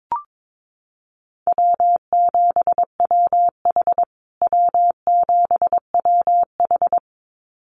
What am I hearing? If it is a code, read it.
Morse code, heard as a keyed tone, about 22 wpm, 700 Hz. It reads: W7WH W7WH
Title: USA Call Signs Repeated 2x 22wpm